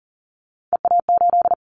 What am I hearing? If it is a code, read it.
Morse code, heard as a keyed tone, about 40 wpm, 690 Hz. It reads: EA8
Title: Callsign Prefixes for Top 100 DXCC - 40wpm